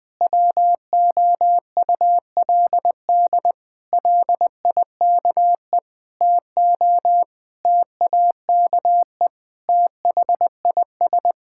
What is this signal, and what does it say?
Morse code, heard as a keyed tone, about 20 wpm, 700 Hz. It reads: WOULD LIKE TO TAKE THIS
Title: Sets of 8 Words - 20wpm